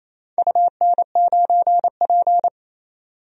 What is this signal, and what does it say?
Morse code, heard as a keyed tone, about 28 wpm, 700 Hz. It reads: UN9P
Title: DXCC-100 Call Signs - 4 Characters - 28wpm